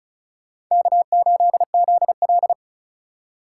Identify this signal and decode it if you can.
Morse code, heard as a keyed tone, about 46 wpm, 700 Hz. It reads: K8ZL